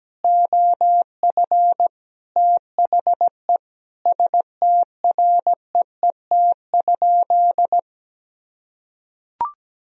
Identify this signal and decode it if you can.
Morse code, heard as a keyed tone, about 17 wpm, 700 Hz. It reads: OF THE STREET?
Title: Sentences from Top 1000 Words - Review Part 3 - 17wpm